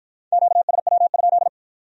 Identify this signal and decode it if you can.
Morse code, heard as a keyed tone, about 53 wpm, 700 Hz. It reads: QSW?